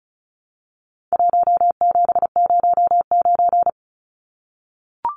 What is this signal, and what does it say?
Morse code, heard as a keyed tone, about 34 wpm, 700 Hz. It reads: 1709
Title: Numbers - Contest Style - 4-Digit 35wpm